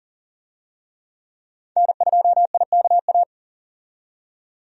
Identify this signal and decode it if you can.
Morse code, heard as a keyed tone, about 40 wpm, 700 Hz. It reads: N2IKA
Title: USA Call Signs - 5 Characters - 40wpm